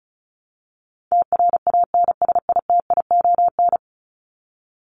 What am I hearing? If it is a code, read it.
Morse code, heard as a keyed tone, about 34 wpm, 700 Hz. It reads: TRANSITION
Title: Top 1751-2000 Words 35wpm